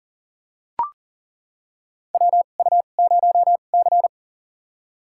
Morse code, heard as a keyed tone, about 40 wpm, spaced as for 29 wpm, 700 Hz. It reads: WU0C
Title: USA Call Signs - 4 Characters - 2x Character Spacing - 40wpm